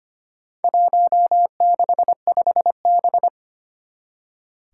Morse code, heard as a keyed tone, about 25 wpm, 700 Hz. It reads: 165B
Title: Four Challenging Letter-Number Combinations 25wpm